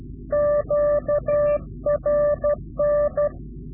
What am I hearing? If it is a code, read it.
Morse code, heard as a keyed tone, about 12 wpm, 590 Hz. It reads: QRN
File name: CW-QSO / Directional Call / Directional QSO